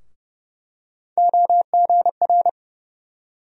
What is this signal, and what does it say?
Morse code, heard as a keyed tone, about 30 wpm, 700 Hz. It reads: OGR
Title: USA Call Sign Postfixes - 3 Letters 30wpm